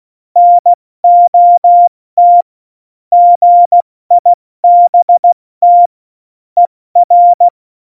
Morse code, heard as a keyed tone, about 16 wpm, 700 Hz. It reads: NOT GIBT ER
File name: Morsekurs / Witze / Witz 09. April 2019, 70 F80